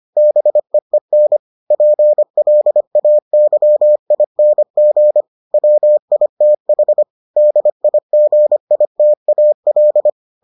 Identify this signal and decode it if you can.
Morse code, heard as a keyed tone, about 25 wpm, 600 Hz. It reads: BEEN PLAYING WITH DIGITAL